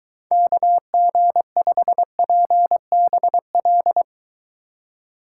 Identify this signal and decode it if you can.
Morse code, heard as a keyed tone, about 21 wpm, 700 Hz. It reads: KG5PBL